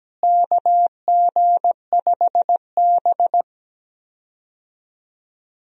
Morse code, heard as a keyed tone, about 17 wpm, 700 Hz. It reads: KG5B
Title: USA Call Signs - 4 Characters - Repeated 2x - 17wpm